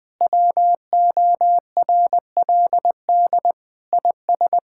Morse code, heard as a keyed tone, about 20 wpm, 700 Hz. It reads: WORLD IS
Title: Sentences with 3-Word Phrases - Spoken Beginning - 20wpm